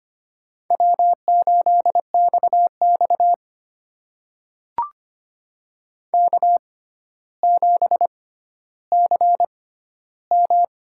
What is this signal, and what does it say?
Morse code, heard as a keyed tone, about 25 wpm, spaced as for 13 wpm, 700 Hz. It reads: W8XX K7CM
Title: USA Call Signs - 4 Characters - 6x Character Spacing - 25wpm